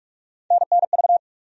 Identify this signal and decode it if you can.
Morse code, heard as a keyed tone, about 46 wpm, 700 Hz. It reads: NNV
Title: USA Call Sign Postfixes - 3 Letters 45wpm